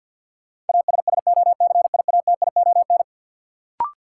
Morse code, heard as a keyed tone, about 52 wpm, 700 Hz. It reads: ASSOCIATION
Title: Sentences from Top 1100 Words - New Word Drill Alt - 50wpm